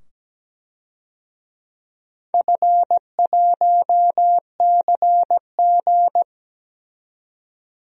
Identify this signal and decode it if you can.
Morse code, heard as a keyed tone, about 17 wpm, 700 Hz. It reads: F1CG
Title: DXCC-100 Call Signs - 4 Characters - 17wpm